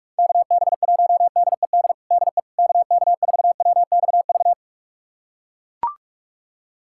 Morse code, heard as a keyed tone, about 41 wpm, 700 Hz. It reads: KB1DED DE KK4WXV
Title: USA Callsign DE Callsign - 6 Characters - 45wpm